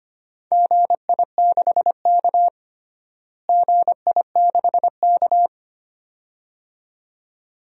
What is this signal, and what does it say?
Morse code, heard as a keyed tone, about 25 wpm, 700 Hz. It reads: GI6K GI6K
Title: DXCC-100 Call Signs - 4 Characters - Repeated 2x - 25wpm